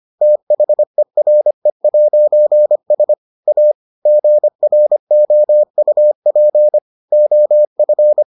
Morse code, heard as a keyed tone, about 25 wpm, 600 Hz. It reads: THERE'S A GROUP OF